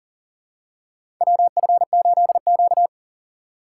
Morse code, heard as a keyed tone, about 40 wpm, 700 Hz. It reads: WF8Q
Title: USA Call Signs - 4 Characters - 6x Character Spacing - 40wpm